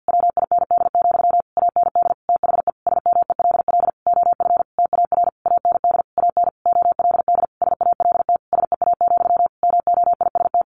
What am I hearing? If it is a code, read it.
Morse code, heard as a keyed tone, about 50 wpm, 700 Hz. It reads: WIND, AND THE SMELL OF TAR AND AN OLD SALT SEA, MOIST
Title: A Sound Of Thunder 50wpm